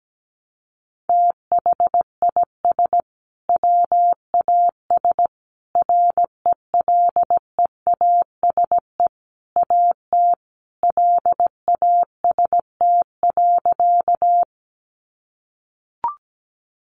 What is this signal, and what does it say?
Morse code, heard as a keyed tone, about 17 wpm, 700 Hz. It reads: THIS WAS RELEASE AT LAST.
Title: Sentences from Top 1100 Words 17wpm